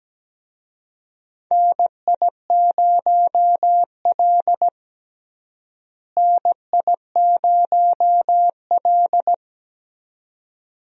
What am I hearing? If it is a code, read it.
Morse code, heard as a keyed tone, about 17 wpm, 700 Hz. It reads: NI0L NI0L
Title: USA Call Signs - 4 Characters - Repeated 2x - 17wpm